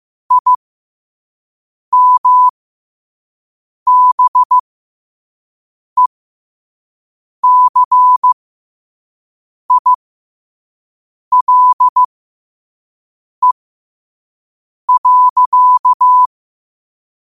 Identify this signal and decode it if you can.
Morse code, heard as a keyed tone, about 15 wpm, spaced as for 5 wpm, 1000 Hz. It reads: IMBECILE.
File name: qotdfiles / 5wpm